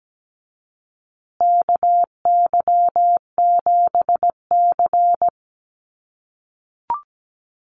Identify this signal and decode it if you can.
Morse code, heard as a keyed tone, about 17 wpm, 700 Hz. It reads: KY7C
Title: USA Call Signs Repeated 3x 17wpm